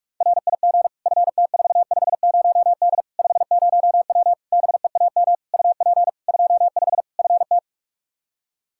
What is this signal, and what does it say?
Morse code, heard as a keyed tone, about 46 wpm, 700 Hz. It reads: RIG FT450D 50W BEAM UP 25 FT
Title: Practice QSO - Exchange 2 - Info - 45wpm